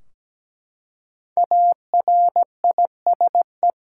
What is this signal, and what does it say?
Morse code, heard as a keyed tone, about 17 wpm, 700 Hz. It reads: ARISE
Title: Five Letter Words - 17wpm